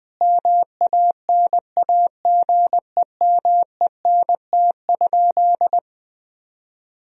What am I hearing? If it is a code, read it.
Morse code, heard as a keyed tone, about 21 wpm, 700 Hz. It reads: MANAGEMENT?
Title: Sentences from Top 1000 Words - Review Part 4 - 20wpm